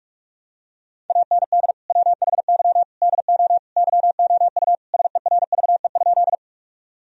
Morse code, heard as a keyed tone, about 46 wpm, 700 Hz. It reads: AND WHY DO YOU SERVE?
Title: Sentences from Top 1000 Words - Review Part 2 - 45wpm